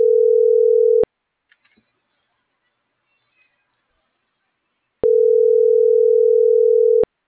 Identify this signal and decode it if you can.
Ringback tone (North America), 440 + 480 Hz.